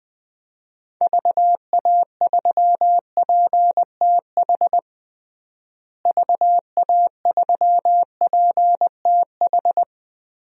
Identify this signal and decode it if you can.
Morse code, heard as a keyed tone, about 20 wpm, 700 Hz. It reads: VA3PTH VA3PTH
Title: DXCC-100 Call Signs - 6 Characters - Repeated 2x - 20wpm